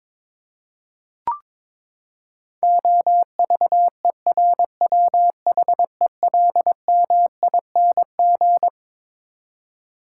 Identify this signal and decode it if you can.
Morse code, heard as a keyed tone, about 21 wpm, 700 Hz. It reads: OVERWHELMING
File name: Twelve Letter Words - 22wpm